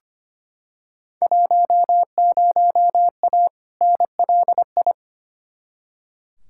Morse code, heard as a keyed tone, about 25 wpm, 700 Hz. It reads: 10A NLI
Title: Field Day Exchanges Encore - 25wpm